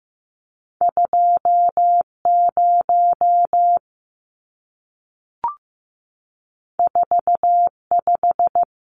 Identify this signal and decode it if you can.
Morse code, heard as a keyed tone, about 15 wpm, 700 Hz. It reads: 20 45
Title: Numbers - 2-Digit 15wpm